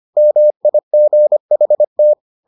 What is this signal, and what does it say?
Morse code, heard as a keyed tone, about 25 wpm, 600 Hz. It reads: MIGHT